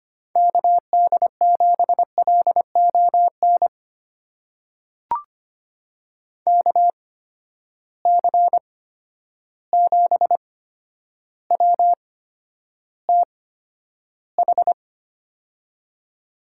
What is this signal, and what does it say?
Morse code, heard as a keyed tone, about 25 wpm, spaced as for 11 wpm, 700 Hz. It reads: KD7LON KC7WTH
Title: USA Call Signs - 6 Characters - 8x Character Spacing - 25wpm